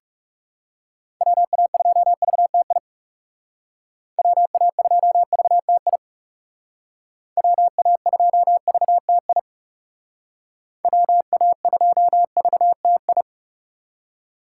Morse code, heard as a keyed tone, about 38 wpm, 700 Hz. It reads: WA2VTI WA2VTI WA2VTI WA2VTI